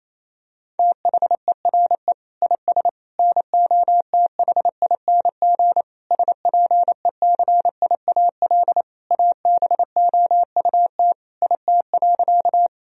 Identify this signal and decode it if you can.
Morse code, heard as a keyed tone, about 28 wpm, 700 Hz. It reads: THERE IS NOTHING SPECIAL ABOUT IT.
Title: Sentences from Top 1000 Words - Review Part 3 - 28wpm